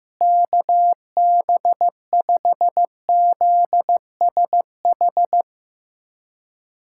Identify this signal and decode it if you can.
Morse code, heard as a keyed tone, about 15 wpm, 700 Hz. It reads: KB5ZSH
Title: USA Callsign DE Callsign - 6 Characters - 15wpm